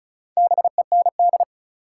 Morse code, heard as a keyed tone, about 36 wpm, 700 Hz. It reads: BEND
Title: Four Letter Words - 35wpm